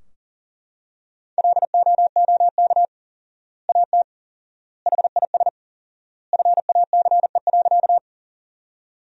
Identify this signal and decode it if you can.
Morse code, heard as a keyed tone, about 40 wpm, 700 Hz. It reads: LOOK AT HIS FACE.
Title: 4x Word-Spacing - Sentences from Top 300 Words - 40wpm